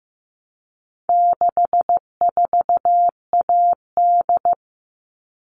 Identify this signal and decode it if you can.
Morse code, heard as a keyed tone, about 15 wpm, 700 Hz. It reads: 64AD
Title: All Cut Numbers - 4-Digit 15wpm